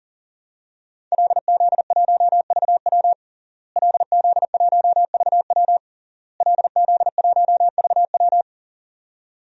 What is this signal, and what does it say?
Morse code, heard as a keyed tone, about 40 wpm, 700 Hz. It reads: LZ1VW LZ1VW LZ1VW
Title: DXCC-100 Call Signs - 5 Characters - Repeated 3x - 40wpm